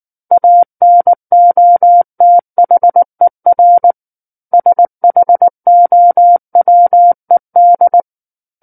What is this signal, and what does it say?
Morse code, heard as a keyed tone, about 19 wpm, 700 Hz. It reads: ANOTHER SHOWED